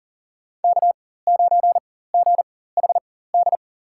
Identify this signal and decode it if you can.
Morse code, heard as a keyed tone, about 38 wpm, spaced as for 18 wpm, 700 Hz. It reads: K9GHD
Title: USA Call Signs - 5 Characters - 4x Character Spacing - 40wpm